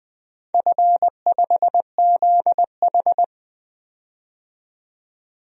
Morse code, heard as a keyed tone, about 20 wpm, 700 Hz. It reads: F5ZH
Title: DXCC-100 Call Signs - 4 Characters - 20wpm